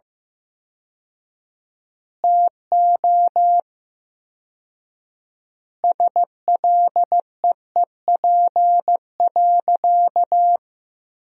Morse code, heard as a keyed tone, about 15 wpm, 700 Hz. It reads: TO SLEEP.
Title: 4x Word-Spacing - Sentences from Top 500 Words - 15wpm